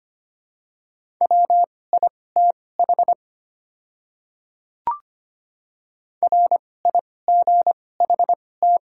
Morse code, heard as a keyed tone, about 25 wpm, spaced as for 18 wpm, 700 Hz. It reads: WITH RIGHT
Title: 2x Character Spacing - Top 300 Words - 25wpm